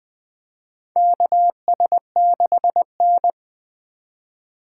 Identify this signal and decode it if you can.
Morse code, heard as a keyed tone, about 21 wpm, 700 Hz. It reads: KS6N
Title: USA Call Signs - 4 Characters - 2x Character Spacing - 20wpm